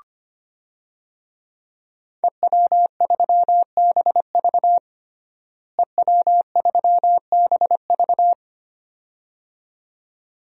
Morse code, heard as a keyed tone, about 25 wpm, 700 Hz. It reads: EW3BV EW3BV
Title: DXCC-100 Call Signs - 5 Characters - Repeated 2x - 25wpm